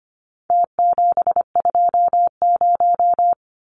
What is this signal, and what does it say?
Morse code, heard as a keyed tone, about 25 wpm, 700 Hz. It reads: T720
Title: Numbers - Contest Style - 4-Digit 25wpm